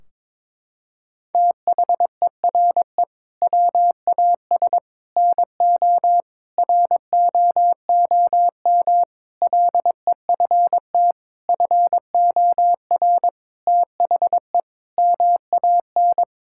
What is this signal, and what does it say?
Morse code, heard as a keyed tone, about 22 wpm, 700 Hz. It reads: THERE WAS NO ROOM LEFT FOR THE MAN
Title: Sentences from Top 1600 Words 22wpm